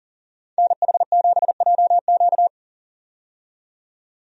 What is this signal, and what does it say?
Morse code, heard as a keyed tone, about 40 wpm, 700 Hz. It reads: NH7JQ
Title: USA Call Signs - 5 Characters - 40wpm